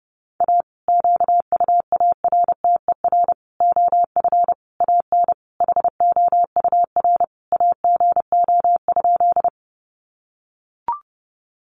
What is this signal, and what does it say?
Morse code, heard as a keyed tone, about 30 wpm, 700 Hz. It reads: A QUARTER OF AN HOUR AGO?
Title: Sentences from Top 1300 Words 30wpm